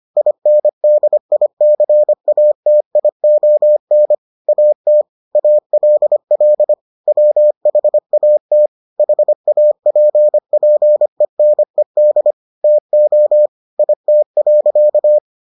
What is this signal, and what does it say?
Morse code, heard as a keyed tone, about 25 wpm, 600 Hz. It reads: INDICATION AT ALL WHAT HAPPENED TO IT.